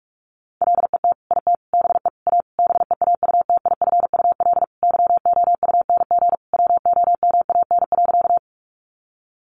Morse code, heard as a keyed tone, about 44 wpm, 700 Hz. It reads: LET IT BE A BEAUTIFUL YOUNG WOMAN.
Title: Sentences from Top 1100 Words 45wpm